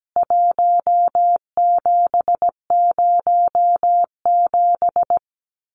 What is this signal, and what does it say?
Morse code, heard as a keyed tone, about 17 wpm, 700 Hz. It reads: 1707
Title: CWT Exchange 17wpm